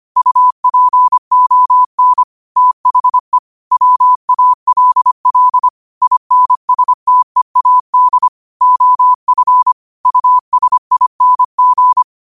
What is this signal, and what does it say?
Morse code, heard as a keyed tone, about 25 wpm, 1000 Hz. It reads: UPON THE WALL INSTEAD OF USING
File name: qotdfiles / 25wpm